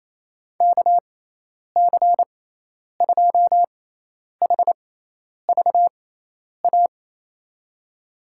Morse code, heard as a keyed tone, about 28 wpm, spaced as for 10 wpm, 700 Hz. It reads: KC2HVA
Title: USA Call Signs - 6 Characters - 6x Character Spacing - 28wpm